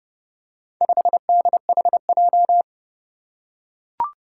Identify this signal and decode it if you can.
Morse code, heard as a keyed tone, about 30 wpm, 700 Hz. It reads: 5DHJ
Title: Four Challenging Letter-Number Combinations 30wpm